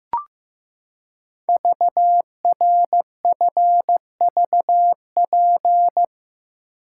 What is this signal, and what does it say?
Morse code, heard as a keyed tone, about 15 wpm, 700 Hz. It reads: VRFVP
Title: Code Groups - 5 Letters - 15wpm